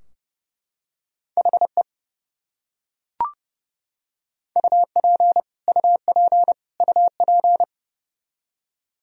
Morse code, heard as a keyed tone, about 31 wpm, 700 Hz. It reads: HE UP UP UP
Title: Top 100 Words Repeated 30wpm